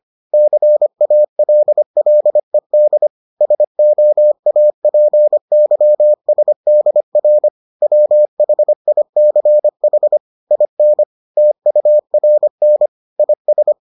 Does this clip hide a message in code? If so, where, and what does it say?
Yes: Morse code from 0.3 to 13.7 s, heard as a keyed tone, about 25 wpm, 600 Hz. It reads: CALLED SOAPYSDR WHICH IN TURN IS